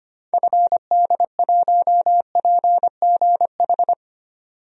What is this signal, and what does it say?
Morse code, heard as a keyed tone, about 25 wpm, 700 Hz. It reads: FD1PGH
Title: DXCC-100 Call Signs - 6 Characters - 25wpm